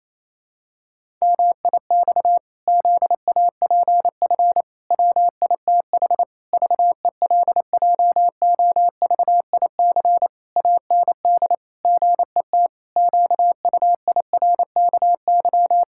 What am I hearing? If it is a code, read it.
Morse code, heard as a keyed tone, about 27 wpm, 700 Hz. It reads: MIX ZAPF WITH VELJOVIC AND GET QUIRKY
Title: Pangram Challenge v2 - 28wpm